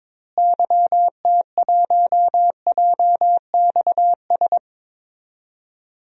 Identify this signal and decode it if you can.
Morse code, heard as a keyed tone, about 22 wpm, 700 Hz. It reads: YT1JXS
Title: DXCC-100 Call Signs - 6 Characters - Repeated 2x - 22wpm